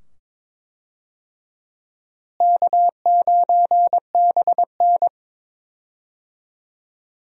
Morse code, heard as a keyed tone, about 22 wpm, 700 Hz. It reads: K9BN
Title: USA Call Signs - 4 Characters - Repeated 2x - 22wpm